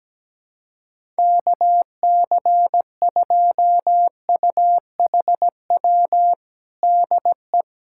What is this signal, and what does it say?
Morse code, heard as a keyed tone, about 17 wpm, 700 Hz. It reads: KC2UHW DE
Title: Practice QSO - Exchange 4 - Ending - 17wpm